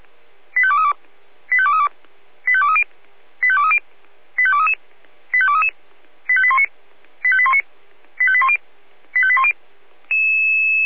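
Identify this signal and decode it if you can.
Five-tone selective call (ZVEI), heard as quick runs of five tones, tones of about 80 ms. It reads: 85232 85232 85230 85230 85233 85233 85819 85819 85810 85810
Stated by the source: ZVEI five-tone selective call received on SDR.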